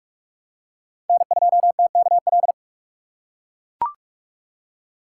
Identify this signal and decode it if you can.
Morse code, heard as a keyed tone, about 44 wpm, 690 Hz. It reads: N2TKL
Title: USA Call Signs - 5 Characters - 6x Character Spacing - 45wpm